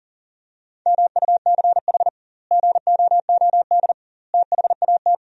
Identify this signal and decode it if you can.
Morse code, heard as a keyed tone, about 37 wpm, 700 Hz. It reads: MUCH GOOD THAT
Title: 2x Word-Spacing - Sentences from Top 200 Words - 40wpm